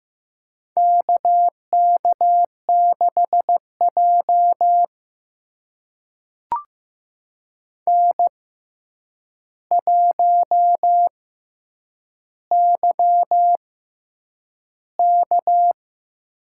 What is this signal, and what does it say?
Morse code, heard as a keyed tone, about 15 wpm, spaced as for 8 wpm, 700 Hz. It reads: KK6J N1YK
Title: USA Call Signs - 4 Characters - 6x Character Spacing - 15wpm